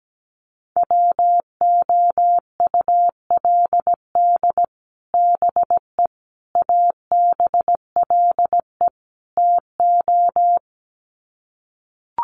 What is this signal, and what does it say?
Morse code, heard as a keyed tone, about 17 wpm, 700 Hz. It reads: WOULD BE ABLE TO
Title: Sets of 4 Words 17wpm